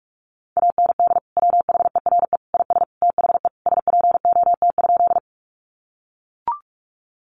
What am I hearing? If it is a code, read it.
Morse code, heard as a keyed tone, about 44 wpm, 700 Hz. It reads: AND WHERE IS THE SPOT?
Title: Sentences from Top 1100 Words 45wpm